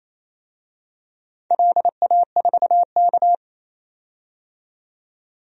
Morse code, heard as a keyed tone, about 27 wpm, 690 Hz. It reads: LA4K
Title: DXCC-100 Call Signs - 4 Characters - 28wpm